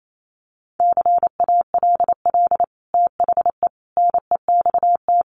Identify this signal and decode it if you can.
Morse code, heard as a keyed tone, about 28 wpm, 700 Hz. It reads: CALL THE NEXT
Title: Sentences from Top 300 Words 28wpm